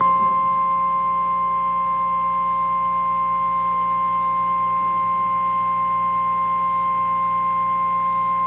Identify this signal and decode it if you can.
Emergency Alert System (SAME) tones (the 1050 Hz attention tone). It is heard from 1.0 to 7.9 s.
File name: eas / KNCYKBIE / KNCYKBIE